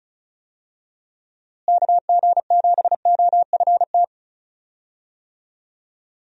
Morse code, heard as a keyed tone, about 34 wpm, 700 Hz. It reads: KG7OFT
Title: USA Call Signs - 6 Characters - Repeated 3x - 35wpm